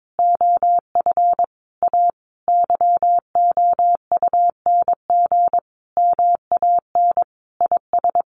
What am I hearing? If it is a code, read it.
Morse code, heard as a keyed tone, about 22 wpm, 700 Hz. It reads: OF A YOUNG MAN IS
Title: Sentences from Top 1300 Words 22wpm